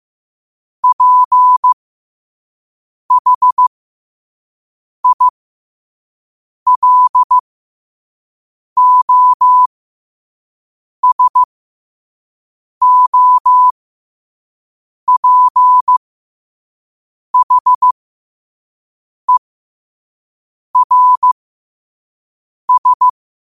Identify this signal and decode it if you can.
Morse code, heard as a keyed tone, about 15 wpm, spaced as for 5 wpm, 1000 Hz. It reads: PHILOSOPHERS
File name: qotdfiles / 5wpm